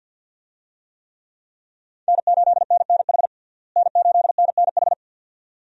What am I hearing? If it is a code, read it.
Morse code, heard as a keyed tone, about 45 wpm, 700 Hz. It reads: N8NNH N8NNH
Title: USA Call Signs - 5 Characters - Repeated 2x - 50wpm